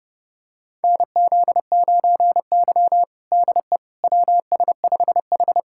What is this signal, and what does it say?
Morse code, heard as a keyed tone, about 30 wpm, 700 Hz. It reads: NZ9Y DE WS5H
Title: USA Callsign DE Callsign - 4 Characters - 30wpm